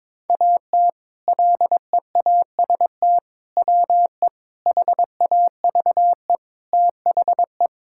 Morse code, heard as a keyed tone, about 22 wpm, 700 Hz. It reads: AT LEAST WE HAVE THE
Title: Sentences from Top 1000 Words - Review Part 5 - 22wpm